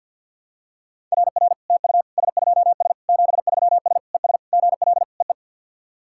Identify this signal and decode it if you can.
Morse code, heard as a keyed tone, about 53 wpm, 690 Hz. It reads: RR TU S2S 73S ES GL EE
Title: SOTA QSO Practice 50wpm